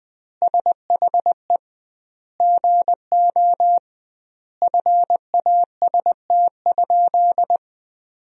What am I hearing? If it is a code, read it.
Morse code, heard as a keyed tone, about 20 wpm, 700 Hz. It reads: SHE GO FAST?
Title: 2x Word-Spacing - Sentences from Top 400 Words - 20wpm